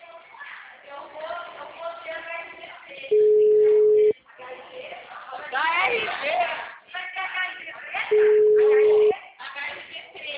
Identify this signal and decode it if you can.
Ringback tone (Europe, 425 Hz), with speech.